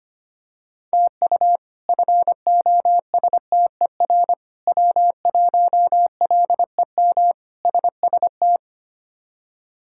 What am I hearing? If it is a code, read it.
Morse code, heard as a keyed tone, about 25 wpm, 700 Hz. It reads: TU FOSTER W1LEM SST
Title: K1USN SST QSO Practice - 25wpm